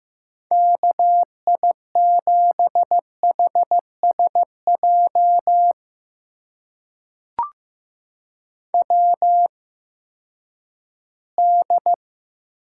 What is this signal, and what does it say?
Morse code, heard as a keyed tone, about 15 wpm, spaced as for 10 wpm, 700 Hz. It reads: KI7HSJ WD
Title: USA Call Signs - 6 Characters - 8x Character Spacing - 15wpm